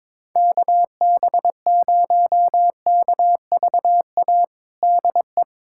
Morse code, heard as a keyed tone, about 22 wpm, 700 Hz. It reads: KB0KVA DE
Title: USA Callsign DE Callsign - 6 Characters - 22wpm